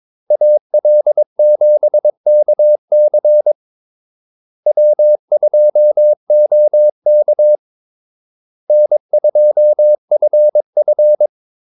Morse code, heard as a keyed tone, about 22 wpm, 600 Hz. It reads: AL7KC W2OK N2FF
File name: Callsigns-22wpm